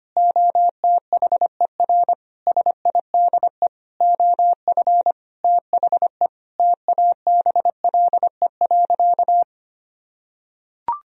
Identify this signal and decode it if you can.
Morse code, heard as a keyed tone, about 25 wpm, 700 Hz. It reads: OTHER SIDE OF THE TABLE.
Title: Sentences from Top 1000 Words - Review Part 4 - 25wpm